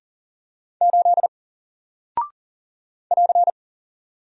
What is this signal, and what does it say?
Morse code, heard as a keyed tone, about 40 wpm, 700 Hz. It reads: 8 <AR>
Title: Single Character (Letter:Number:Punctuation:Prosign) Alt 40wpm